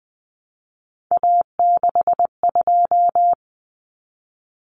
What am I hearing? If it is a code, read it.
Morse code, heard as a keyed tone, about 20 wpm, 700 Hz. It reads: A62
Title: Common Cut Numbers - 3-Digit 20wpm